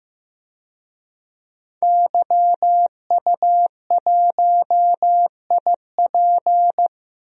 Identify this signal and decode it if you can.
Morse code, heard as a keyed tone, about 15 wpm, 700 Hz. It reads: YU1IP
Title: DXCC-100 Call Signs - 5 Characters - Repeated 3x - 15wpm